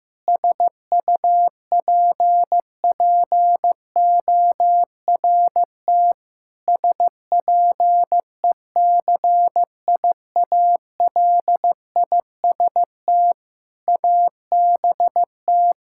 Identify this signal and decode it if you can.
Morse code, heard as a keyed tone, about 15 wpm, 700 Hz. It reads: SUPPORT SPECIALIST ABT
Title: Practice QSO - Exchange 3 - More Info - 15wpm